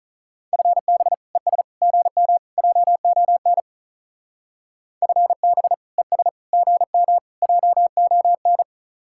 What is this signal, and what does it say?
Morse code, heard as a keyed tone, about 44 wpm, 700 Hz. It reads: FB ES GM JON FB ES GM JON
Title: Speed-Racing Common QSO Phrases Encore - 35wpm